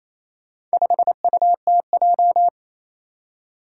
Morse code, heard as a keyed tone, about 27 wpm, 700 Hz. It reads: 5UTJ
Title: Four Challenging Letter-Number Combinations 28wpm